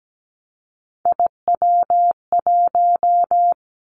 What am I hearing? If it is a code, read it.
Morse code, heard as a keyed tone, about 17 wpm, 700 Hz. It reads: IW1
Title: Callsign Prefixes for Top 50 DXCC - 17wpm